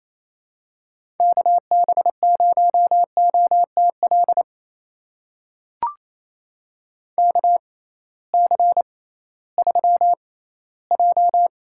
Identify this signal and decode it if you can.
Morse code, heard as a keyed tone, about 28 wpm, spaced as for 16 wpm, 700 Hz. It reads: KB0OTL KC3J
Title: USA Call Signs - 6 Characters - 6x Character Spacing - 28wpm